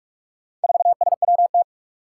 Morse code, heard as a keyed tone, about 44 wpm, 700 Hz. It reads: 4SWT
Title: Four Challenging Letter-Number Combinations 45wpm